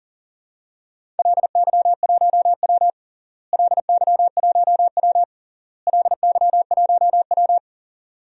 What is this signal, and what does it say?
Morse code, heard as a keyed tone, about 40 wpm, 700 Hz. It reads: LY1W LY1W LY1W
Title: DXCC-100 Call Signs - 4 Characters - Repeated 3x - 40wpm